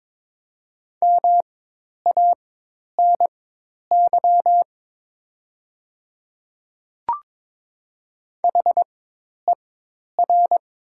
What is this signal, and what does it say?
Morse code, heard as a keyed tone, about 22 wpm, spaced as for 10 wpm, 700 Hz. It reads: MANY HER
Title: 4x Character Spacing - Top 100 Words - 22wpm